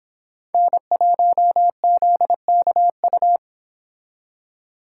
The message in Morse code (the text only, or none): N1ZKU